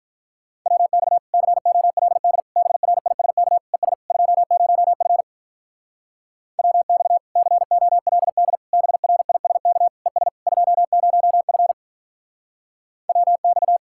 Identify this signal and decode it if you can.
Morse code, heard as a keyed tone, about 44 wpm, 700 Hz. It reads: WX COLD BRISK ES 20F WX COLD BRISK ES 20F WX
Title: Speed-Racing Common QSO Phrases - 35wpm